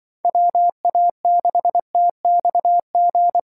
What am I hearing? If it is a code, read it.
Morse code, heard as a keyed tone, about 23 wpm, 700 Hz. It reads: WA6TXG
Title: Speed Racing - USA 2x3 Call Signs 20wpm